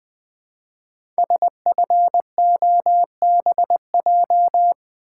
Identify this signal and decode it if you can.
Morse code, heard as a keyed tone, about 20 wpm, 700 Hz. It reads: SFOBJ
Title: Code Groups - 5 Letters - 20wpm